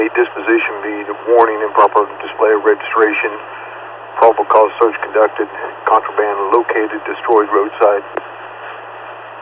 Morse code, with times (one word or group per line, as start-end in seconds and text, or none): none